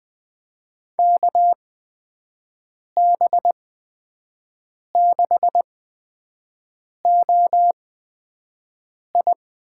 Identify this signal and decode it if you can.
Morse code, heard as a keyed tone, about 20 wpm, spaced as for 5 wpm, 700 Hz. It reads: KB6OI
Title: USA Call Signs - 6 Characters - 8x Character Spacing - 20wpm